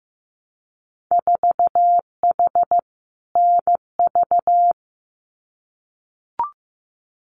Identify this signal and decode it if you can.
Morse code, heard as a keyed tone, about 15 wpm, 700 Hz. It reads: 4H NV
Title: Winter Field Day Exchanges Encore 15wpm